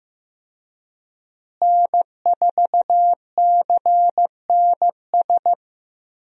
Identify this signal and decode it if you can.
Morse code, heard as a keyed tone, about 15 wpm, 700 Hz. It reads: N4CNS
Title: USA Call Signs - 5 Characters - Repeated 2x - 15wpm